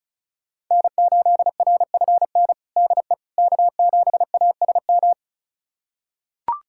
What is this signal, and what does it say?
Morse code, heard as a keyed tone, about 36 wpm, 700 Hz. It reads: N8RFN DE K7ASM
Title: USA Callsign DE Callsign - 5 Characters - 35wpm